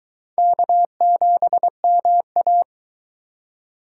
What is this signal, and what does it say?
Morse code, heard as a keyed tone, about 23 wpm, 700 Hz. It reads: K7MA